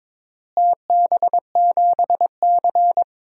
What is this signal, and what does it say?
Morse code, heard as a keyed tone, about 22 wpm, 700 Hz. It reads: TB7C
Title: DXCC-100 Call Signs - 4 Characters - Repeated 3x - 22wpm